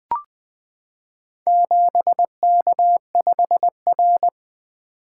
Morse code, heard as a keyed tone, about 20 wpm, 700 Hz. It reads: 7K5R
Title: Four Letter-Number Combinations - 20wpm